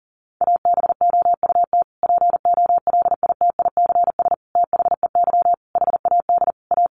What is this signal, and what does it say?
Morse code, heard as a keyed tone, about 40 wpm, 700 Hz. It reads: ABOUT POLITICS THEY HAD A
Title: Sentences from Top 1000 Words 40wpm